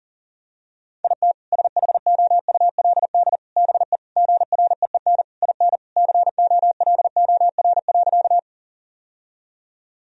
Morse code, heard as a keyed tone, about 40 wpm, 700 Hz. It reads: IT SHOULD BE GREEN IN COLOR.
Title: Sentences from Top 1000 Words - Review Part 2 - 40wpm